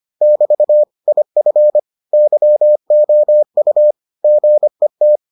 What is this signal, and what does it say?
Morse code, heard as a keyed tone, about 25 wpm, 600 Hz. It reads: <BT> IF YOU GET